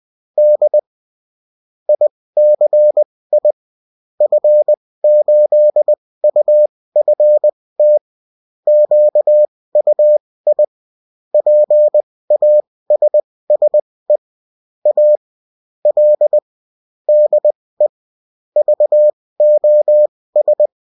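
Morse code, heard as a keyed tone, about 20 wpm, spaced as for 16 wpm, 600 Hz. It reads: D ICI F8UFT QUI PASSE A L DE VOS
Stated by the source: Bulletin in text and CW form.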